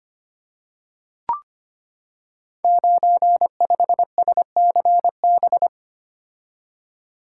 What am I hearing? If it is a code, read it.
Morse code, heard as a keyed tone, about 25 wpm, 700 Hz. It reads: 95SCB
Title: Code Groups - 5 Letter:Numbers - 25wpm